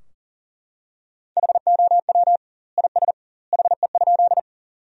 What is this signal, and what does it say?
Morse code, heard as a keyed tone, about 40 wpm, 700 Hz. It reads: HOW IS HE?
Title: 2x Word-Spacing - Sentences from Top 100 Words - 40wpm